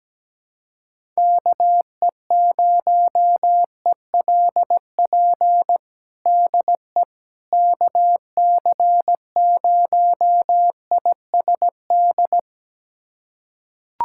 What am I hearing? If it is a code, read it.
Morse code, heard as a keyed tone, about 17 wpm, 700 Hz. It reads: KE0ELP DE KC0ISD
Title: USA Callsign DE Callsign - 6 Characters - 17wpm